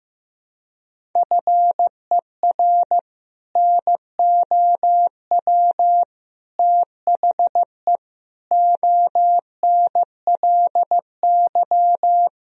Morse code, heard as a keyed tone, about 15 wpm, 700 Hz. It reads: FER NOW THE ONLY
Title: Sentences from Top 200 Words and Abbreviations 15wpm